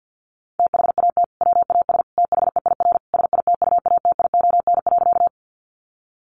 Morse code, heard as a keyed tone, about 49 wpm, 700 Hz. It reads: THAT WAS THEIR SITUATION.
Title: Sentences from Top 1200 Words 50wpm